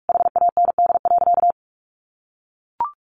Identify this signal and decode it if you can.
Morse code, heard as a keyed tone, about 44 wpm, 700 Hz. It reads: HAND.
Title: Sentences from Top 1400 Words 45wpm